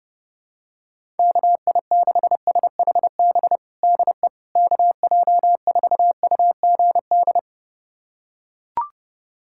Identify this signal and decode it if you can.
Morse code, heard as a keyed tone, about 30 wpm, 700 Hz. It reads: KI6SHB DE KJ4UGD
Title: USA Callsign DE Callsign 30wpm